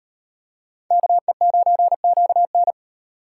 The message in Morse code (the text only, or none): KE9QN